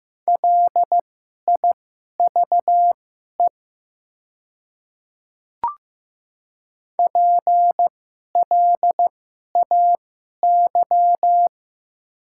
Morse code, heard as a keyed tone, about 15 wpm, spaced as for 11 wpm, 700 Hz. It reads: LIVE PLAY
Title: 2x Character Spacing - Top 300 Words - 15wpm